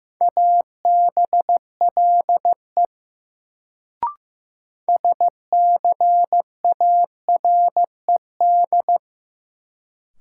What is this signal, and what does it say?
Morse code, heard as a keyed tone, about 15 wpm, 700 Hz. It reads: ABLE SCARED
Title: Sentences from Top 700 Words - New Word Drill - 15wpm